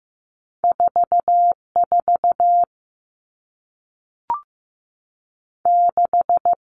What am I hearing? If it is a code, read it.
Morse code, heard as a keyed tone, about 15 wpm, 700 Hz. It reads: 4 4 6
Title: Two Challenging Letter-Number Combinations 15wpm